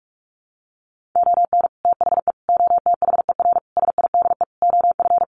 Morse code, heard as a keyed tone, about 44 wpm, 700 Hz. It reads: ON THE OTHER SIDE OF
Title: Sets of 5 Words 45wpm